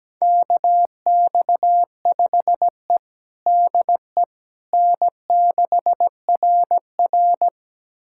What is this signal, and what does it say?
Morse code, heard as a keyed tone, about 17 wpm, 700 Hz. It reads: KX5E DE N6RR
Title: USA Callsign DE Callsign - 4 Characters - 17wpm